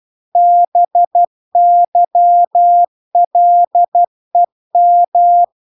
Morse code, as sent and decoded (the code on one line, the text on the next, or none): -... -.-- .-.. --
BYLEM